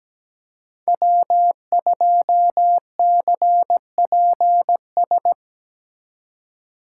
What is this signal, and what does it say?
Morse code, heard as a keyed tone, about 20 wpm, 700 Hz. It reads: W2CPS
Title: Speed Racing - USA 1x3 Call Signs 17wpm